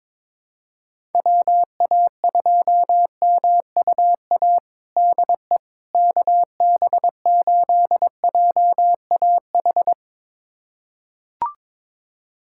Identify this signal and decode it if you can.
Morse code, heard as a keyed tone, about 22 wpm, 700 Hz. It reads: WA2MUA DE KB8JAH
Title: USA Callsign DE Callsign 22wpm